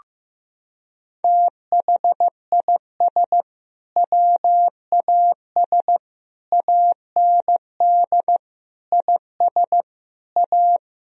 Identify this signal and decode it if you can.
Morse code, heard as a keyed tone, about 16 wpm, 700 Hz. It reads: THIS WAS AND IS A
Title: Sentences from Top 1000 Words - Review Part 4 - 15wpm